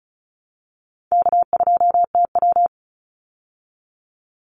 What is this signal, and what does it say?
Morse code, heard as a keyed tone, about 34 wpm, 700 Hz. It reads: K2TW
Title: Easy Call Signs Repeated 3x - 35wpm